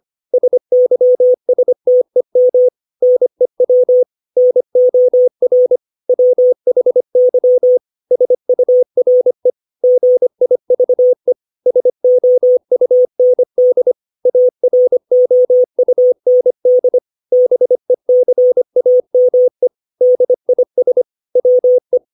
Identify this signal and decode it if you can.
Morse code, heard as a keyed tone, about 25 wpm, 500 Hz. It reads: SYSTEM NEW NOR WHY SURE GIVE SOUND AROUND BECAME DIS WE
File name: 632 words / 632 Words 25WPM